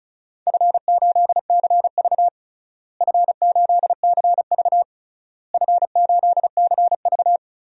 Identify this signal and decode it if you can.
Morse code, heard as a keyed tone, about 36 wpm, 700 Hz. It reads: F8CV F8CV F8CV
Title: DXCC-100 Call Signs - 4 Characters - Repeated 3x - 35wpm